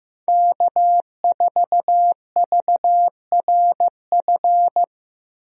Morse code, heard as a keyed tone, about 15 wpm, 700 Hz. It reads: K4VRF
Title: USA Call Signs - 5 Characters - Repeated 2x - 15wpm